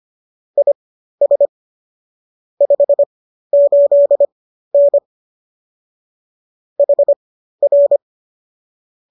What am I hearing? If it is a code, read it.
Morse code, heard as a keyed tone, about 25 wpm, spaced as for 13 wpm, 570 Hz. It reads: IS 58N HR